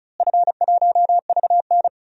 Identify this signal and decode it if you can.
Morse code, heard as a keyed tone, about 36 wpm, 700 Hz. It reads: F1VN
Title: DXCC-100 Call Signs - 4 Characters - Repeated 3x - 35wpm